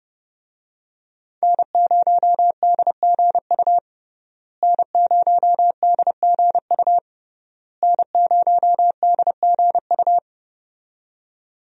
Morse code, heard as a keyed tone, about 30 wpm, 700 Hz. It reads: N0DGU N0DGU N0DGU
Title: USA Call Signs - 5 Characters - Repeated 3x - 30wpm